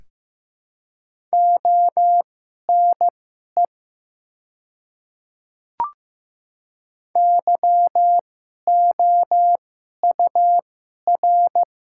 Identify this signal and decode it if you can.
Morse code, heard as a keyed tone, about 15 wpm, spaced as for 11 wpm, 700 Hz. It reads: ONE YOUR
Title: 2x Character Spacing - Top 100 Words - 15wpm